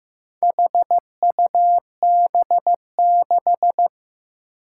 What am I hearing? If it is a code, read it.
Morse code, heard as a keyed tone, about 15 wpm, 700 Hz. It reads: HUB6
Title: Four Challenging Letter-Number Combinations 15wpm